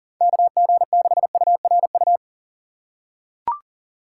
Morse code, heard as a keyed tone, about 40 wpm, 700 Hz. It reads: KG6URU
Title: USA Call Signs - 6 Characters - 3x Character Spacing - 40wpm